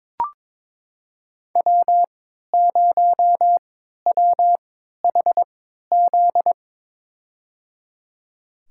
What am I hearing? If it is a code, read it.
Morse code, heard as a keyed tone, about 22 wpm, spaced as for 12 wpm, 700 Hz. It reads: W0WHZ
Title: USA Call Signs - 5 Characters - 3x Character Spacing - 22wpm